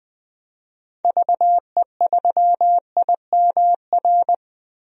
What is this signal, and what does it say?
Morse code, heard as a keyed tone, about 20 wpm, 700 Hz. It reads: VE3IMR
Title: DXCC-100 Call Signs - 6 Characters - 20wpm